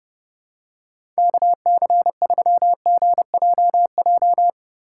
Morse code, heard as a keyed tone, about 30 wpm, 700 Hz. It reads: KC3GJJ
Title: USA Call Signs - 6 Characters - 4x Character Spacing - 30wpm